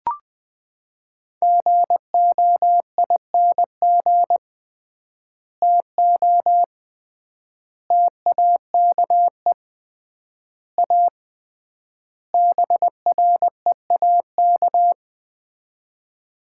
Morse code, heard as a keyed tone, about 20 wpm, 700 Hz. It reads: GOING TO TAKE A BREAK
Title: 3x Word-Spacing - Sets of 5 Words - 20wpm